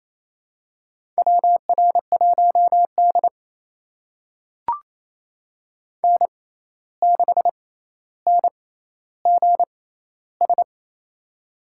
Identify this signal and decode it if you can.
Morse code, heard as a keyed tone, about 28 wpm, spaced as for 13 wpm, 700 Hz. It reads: WR1D N6NGS
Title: USA Call Signs - 6x Character Spacing - 28wpm